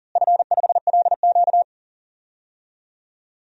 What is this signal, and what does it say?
Morse code, heard as a keyed tone, about 40 wpm, 700 Hz. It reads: F5LQ
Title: DXCC-100 Call Signs - 4 Characters - Repeated 3x - 40wpm